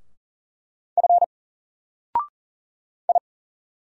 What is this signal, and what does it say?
Morse code, heard as a keyed tone, about 40 wpm, 700 Hz. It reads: F I
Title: Single Character (Letter:Number:Punctuation:Prosign) 40wpm